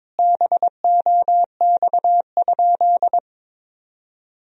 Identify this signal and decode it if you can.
Morse code, heard as a keyed tone, about 22 wpm, 700 Hz. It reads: BOX?
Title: Sentences from Top 1000 Words - Review Part 2 - 22wpm